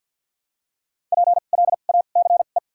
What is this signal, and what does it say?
Morse code, heard as a keyed tone, about 54 wpm, 700 Hz. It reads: P L A C E
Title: 2x Character Spacing - Top 500 Words - 50wpm